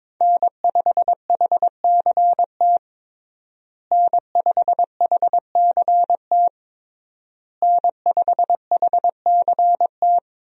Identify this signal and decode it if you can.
Morse code, heard as a keyed tone, about 22 wpm, 700 Hz. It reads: N5HCT N5HCT N5HCT
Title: USA Call Signs - 5 Characters - Repeated 3x - 22wpm